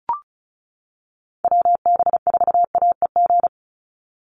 Morse code, heard as a keyed tone, about 36 wpm, 700 Hz. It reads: WB4AEG
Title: Call Signs - 35wpm